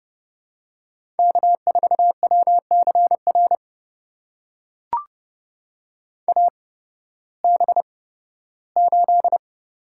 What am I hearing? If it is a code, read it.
Morse code, heard as a keyed tone, about 30 wpm, spaced as for 16 wpm, 700 Hz. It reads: K4WCR AB8
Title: USA Call Signs - 8x Character Spacing - 30wpm